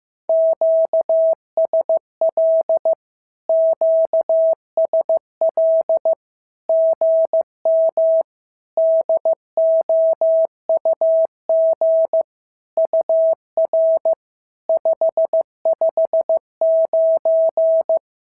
Morse code, heard as a keyed tone, about 15 wpm, 650 Hz. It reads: QSL QSL GM DOUG UR 559
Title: SOTA QSO Practice 15wpm